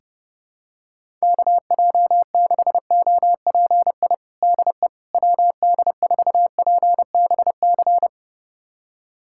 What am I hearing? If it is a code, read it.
Morse code, heard as a keyed tone, about 30 wpm, 700 Hz. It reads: KJ6OPI DE WD4PBC
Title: USA Callsign DE Callsign - 6 Characters - 30wpm